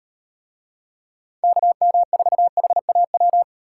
Morse code, heard as a keyed tone, about 38 wpm, 700 Hz. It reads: KM4HAW